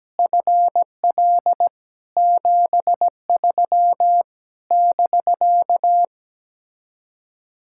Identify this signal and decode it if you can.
Morse code, heard as a keyed tone, about 17 wpm, 700 Hz. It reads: FL 73 <BK>
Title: Speed-Racing Common QSO Phrases - 17wpm